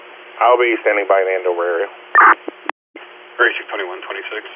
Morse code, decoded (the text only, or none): none